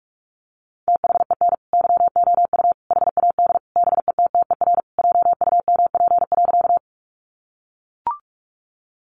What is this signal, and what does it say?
Morse code, heard as a keyed tone, about 44 wpm, 700 Hz. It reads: THEN YOU HAD BETTER JUMP.
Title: Sentences from Top 1300 Words 45wpm